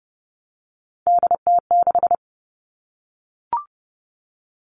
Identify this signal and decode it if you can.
Morse code, heard as a keyed tone, about 30 wpm, 700 Hz. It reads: DT6
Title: All Cut Numbers - 3-Digit 30wpm